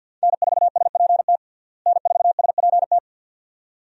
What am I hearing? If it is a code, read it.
Morse code, heard as a keyed tone, about 50 wpm, 700 Hz. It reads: N4SPT N4SPT
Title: USA Call Signs - 5 Characters - Repeated 2x - 50wpm